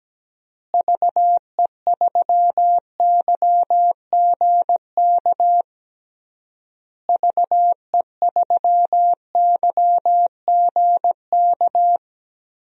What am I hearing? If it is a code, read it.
Morse code, heard as a keyed tone, about 17 wpm, 700 Hz. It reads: VE3YGK VE3YGK
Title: DXCC-100 Call Signs - 6 Characters - Repeated 3x - 17wpm